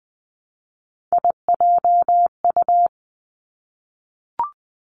Morse code, heard as a keyed tone, about 20 wpm, 700 Hz. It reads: IJU
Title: Three Challenging Letter-Number Combinations 20wpm